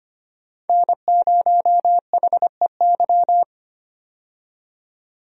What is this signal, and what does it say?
Morse code, heard as a keyed tone, about 25 wpm, 700 Hz. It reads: N0HEY
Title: USA Call Signs - 5 Characters - 25wpm